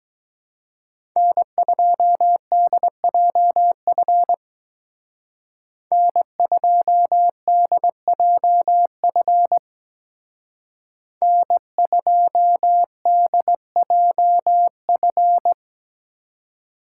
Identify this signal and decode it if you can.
Morse code, heard as a keyed tone, about 20 wpm, 700 Hz. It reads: N2DJF N2DJF N2DJF
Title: Speed Racing - USA Call Signs 17wpm